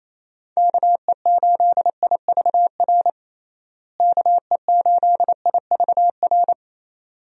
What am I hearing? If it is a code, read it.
Morse code, heard as a keyed tone, about 27 wpm, 700 Hz. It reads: KE8IVR KE8IVR
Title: USA Call Signs - 6 Characters - Repeated 2x - 28wpm